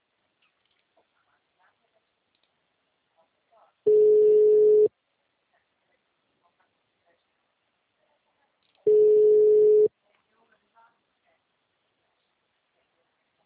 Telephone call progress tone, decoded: ringback tone

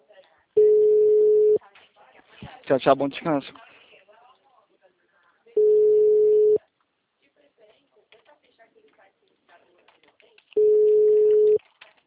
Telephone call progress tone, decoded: ringback tone

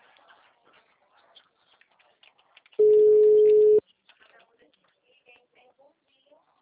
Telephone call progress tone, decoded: ringback tone

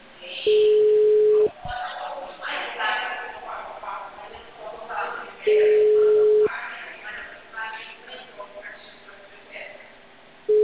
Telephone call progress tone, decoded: ringback tone

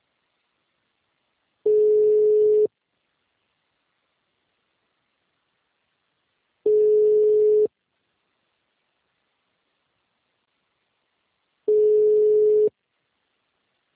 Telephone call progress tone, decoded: ringback tone

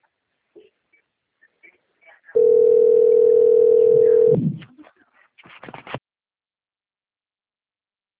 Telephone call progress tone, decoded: ringback tone

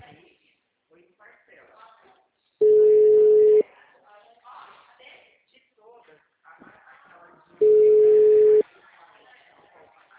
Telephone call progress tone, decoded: ringback tone

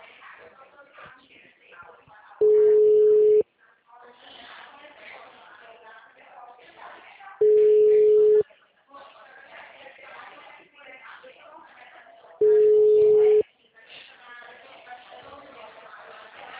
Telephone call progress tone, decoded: ringback tone

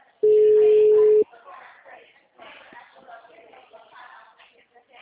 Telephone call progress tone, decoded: ringback tone

